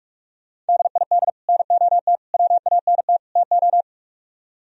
Morse code, heard as a keyed tone, about 62 wpm, spaced as for 53 wpm, 700 Hz. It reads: DID NOT WANT TO
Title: Speed-Racing - Sets of 4 Words 45wpm